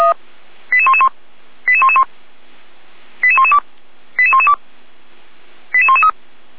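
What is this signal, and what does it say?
Five-tone selective call (ZVEI), heard as quick runs of five tones, tones of about 70 ms. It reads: 88111 88111 88112 88112 88113
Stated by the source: Five-tone ZVEI selective call received by SDR.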